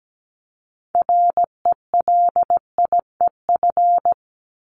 Morse code, heard as a keyed tone, about 17 wpm, 700 Hz. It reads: RELIEF
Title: Top 1501-1750 Words 17wpm